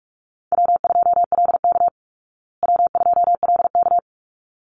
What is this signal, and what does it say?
Morse code, heard as a keyed tone, about 44 wpm, 700 Hz. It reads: W2LK W2LK
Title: Easy Call Signs Repeated 2x - 45wpm